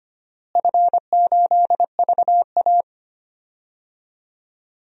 Morse code, heard as a keyed tone, about 25 wpm, 700 Hz. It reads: F8VA
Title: DXCC-100 Call Signs - 4 Characters - Repeated 2x - 25wpm